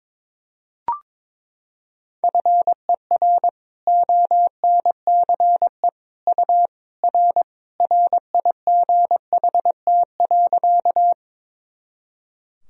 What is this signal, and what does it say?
Morse code, heard as a keyed tone, about 21 wpm, 700 Hz. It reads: FER ONCE U R RIGHT.
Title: Sentences from Top 300 Words and Abbreviations 22wpm